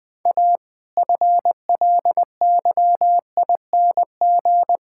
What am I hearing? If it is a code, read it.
Morse code, heard as a keyed tone, about 21 wpm, 700 Hz. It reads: A FLYING
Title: All-Age Appropriate Jokes - 20wpm